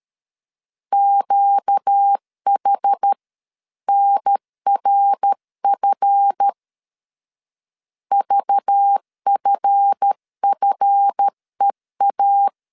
Morse code, heard as a keyed tone, about 13 wpm, 800 Hz. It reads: QH NRF VFFEA
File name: LEZIONI / 7 GF9Z / 7 GF 12wpm 2122